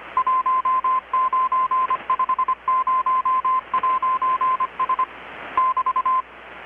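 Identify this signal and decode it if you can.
Morse code, heard as a keyed tone, about 25 wpm, 1040 Hz. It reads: 1950'S <BT>
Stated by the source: Morse from KPH's Night of Nights XX maritime radiotelegraphy event, received in Chiapas, Mexico about 3,800 km away on a Kenwood R-600 with a grounded 10 m long-wire antenna.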